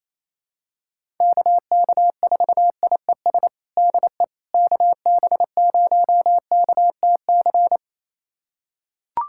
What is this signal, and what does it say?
Morse code, heard as a keyed tone, about 28 wpm, 700 Hz. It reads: KK4IES DE KB0KTC
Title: USA Callsign DE Callsign 28wpm